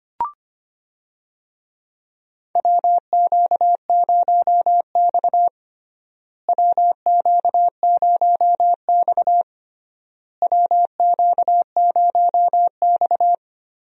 Morse code, heard as a keyed tone, about 25 wpm, 700 Hz. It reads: WQ0X WQ0X WQ0X
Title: USA Call Signs - 4 Characters - Repeated 3x - 25wpm